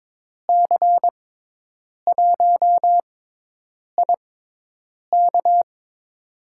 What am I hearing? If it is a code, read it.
Morse code, heard as a keyed tone, about 22 wpm, spaced as for 8 wpm, 700 Hz. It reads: C1IK
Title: USA Call Signs - 6x Character Spacing - 22wpm